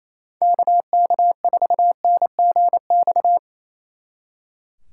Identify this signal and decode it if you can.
Morse code, heard as a keyed tone, about 34 wpm, 700 Hz. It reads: KK4NGX
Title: Speed Racing - USA 2x3 Call Signs 28wpm